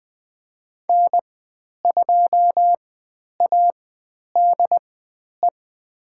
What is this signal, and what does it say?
Morse code, heard as a keyed tone, about 20 wpm, spaced as for 10 wpm, 700 Hz. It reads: N2ADE
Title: 102 Lesson 17g - Letter u - Call Signs 20wpm